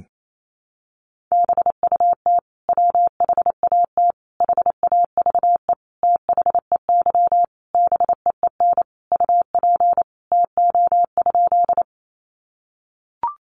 Morse code, heard as a keyed tone, about 28 wpm, 700 Hz. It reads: BUT WHAT HAVE THEY BEEN UP TO?
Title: Sentences from Top 100 Words 28wpm